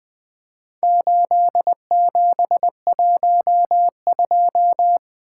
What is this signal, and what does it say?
Morse code, heard as a keyed tone, about 20 wpm, 700 Hz. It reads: 8712
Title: Four Letter-Number Combinations - 20wpm